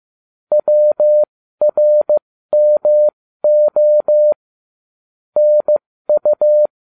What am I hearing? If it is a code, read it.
Morse code, heard as a keyed tone, about 15 wpm, spaced as for 13 wpm, 600 Hz. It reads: WRMO NU